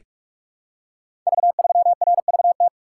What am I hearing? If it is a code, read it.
Morse code, heard as a keyed tone, about 44 wpm, 700 Hz. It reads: V3RVT
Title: Code Groups - 5 Letter:Numbers - 45wpm